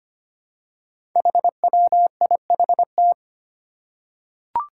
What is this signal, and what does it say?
Morse code, heard as a keyed tone, about 25 wpm, 700 Hz. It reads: HWIHT
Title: Five Challenging Letter-Number Combinations 25wpm